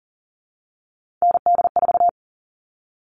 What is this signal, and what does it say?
Morse code, heard as a keyed tone, about 40 wpm, 700 Hz. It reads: ND4
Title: All Cut Numbers - 3-Digit 40wpm